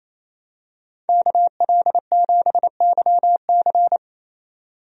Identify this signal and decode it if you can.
Morse code, heard as a keyed tone, about 28 wpm, 700 Hz. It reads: KL7YC
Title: DXCC-100 Call Signs - 5 Characters - 28wpm